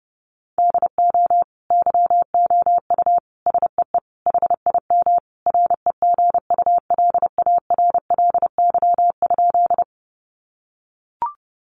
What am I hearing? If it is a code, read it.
Morse code, heard as a keyed tone, about 30 wpm, 700 Hz. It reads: DO YOU SEE HIM REGULARLY?
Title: Sentences from Top 1400 Words 30wpm